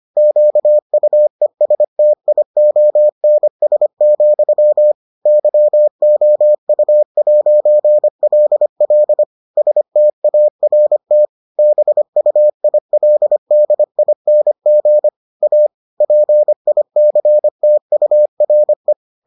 Morse code, heard as a keyed tone, about 25 wpm, 600 Hz. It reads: QUESTIONS, YOU'LL START BUILDING A PICTURE